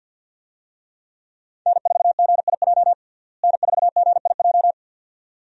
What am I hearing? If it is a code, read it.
Morse code, heard as a keyed tone, about 53 wpm, 700 Hz. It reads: N4GIJ N4GIJ
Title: USA Call Signs - 5 Characters - Repeated 2x - 50wpm